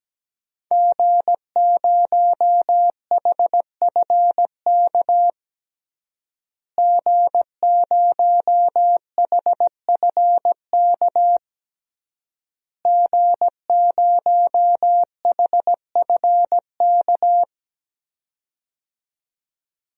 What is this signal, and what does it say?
Morse code, heard as a keyed tone, about 17 wpm, 700 Hz. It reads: G0HFK G0HFK G0HFK
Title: DXCC-100 Call Signs - 5 Characters - Repeated 3x - 17wpm